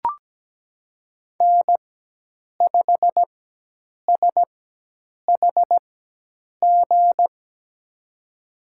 Morse code, heard as a keyed tone, about 17 wpm, spaced as for 8 wpm, 700 Hz. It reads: N5SHG
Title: USA Call Signs - 5 Characters - 4x Character Spacing - 17wpm